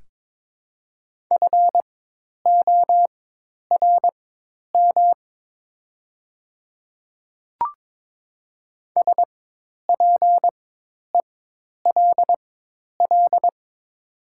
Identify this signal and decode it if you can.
Morse code, heard as a keyed tone, about 22 wpm, spaced as for 10 wpm, 700 Hz. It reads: FORM SPELL
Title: 4x Character Spacing - Top 200 Words - 22wpm